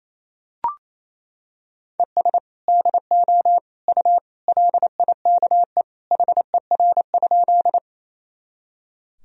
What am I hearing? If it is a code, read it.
Morse code, heard as a keyed tone, about 28 wpm, 700 Hz. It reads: ES DO U LIKE HER?
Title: Sentences from Top 100 Words and Abbreviations 28wpm